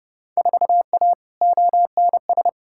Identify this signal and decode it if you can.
Morse code, heard as a keyed tone, about 30 wpm, 700 Hz. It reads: 4A ONS
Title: Field Day Exchanges - 30wpm